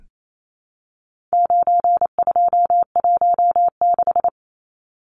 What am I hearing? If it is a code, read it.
Morse code, heard as a keyed tone, about 28 wpm, 700 Hz. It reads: 9216
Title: Numbers - 4-Digit 28wpm